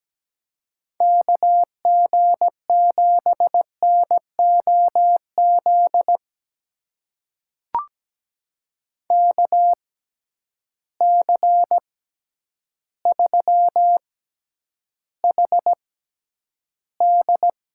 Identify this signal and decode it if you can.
Morse code, heard as a keyed tone, about 17 wpm, spaced as for 9 wpm, 700 Hz. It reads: KG7NOZ KC3HD
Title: USA Call Signs - 6 Characters - 6x Character Spacing - 17wpm